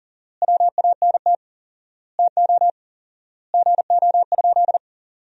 Morse code, heard as a keyed tone, about 40 wpm, 700 Hz. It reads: WANT TO GO?
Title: 4x Word-Spacing - Sentences from Top 300 Words - 40wpm